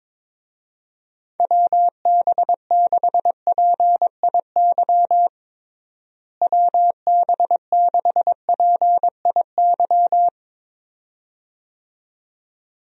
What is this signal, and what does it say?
Morse code, heard as a keyed tone, about 22 wpm, 700 Hz. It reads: WB6PIY WB6PIY
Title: USA Call Signs - 6 Characters - Repeated 2x - 22wpm